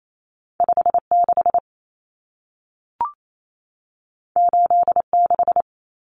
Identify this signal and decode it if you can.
Morse code, heard as a keyed tone, about 28 wpm, 700 Hz. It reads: 56 86
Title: Numbers - 2-Digit 28wpm